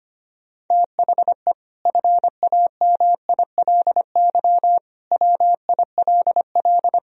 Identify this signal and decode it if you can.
Morse code, heard as a keyed tone, about 25 wpm, 700 Hz. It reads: THE FAMILY WILL
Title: Sentences from Top 1000 Words - Review Part 2 - 25wpm